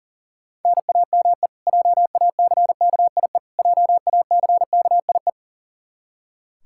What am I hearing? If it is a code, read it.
Morse code, heard as a keyed tone, about 37 wpm, 700 Hz. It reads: NAME JACKIE JACKIE
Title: Practice QSO - Exchange 1 - Beginning - 40wpm